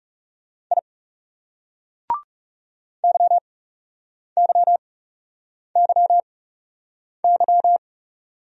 Morse code, heard as a keyed tone, about 37 wpm, spaced as for 8 wpm, 700 Hz. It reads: I YYYY
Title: Speed-Racing Single Letter-Number 30wpm